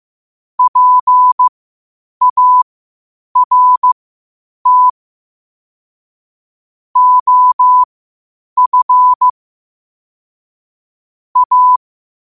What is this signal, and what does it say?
Morse code, heard as a keyed tone, about 15 wpm, spaced as for 8 wpm, 1000 Hz. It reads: PART OF A
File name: qotdfiles / 8wpm